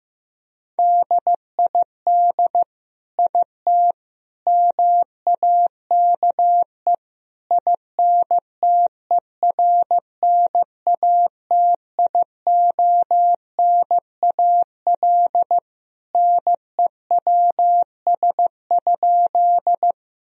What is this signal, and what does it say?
Morse code, heard as a keyed tone, about 15 wpm, 700 Hz. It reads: DID IT MAKE INTERNATIONAL NEWS?
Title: Sentences from Top 1000 Words - Review Part 4 - 15wpm